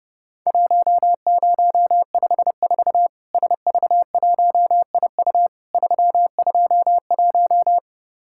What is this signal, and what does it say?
Morse code, heard as a keyed tone, about 30 wpm, 700 Hz. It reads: 1054 SV1IU 321
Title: Worked All Europe DX Contest - QTC Exchange - 30wpm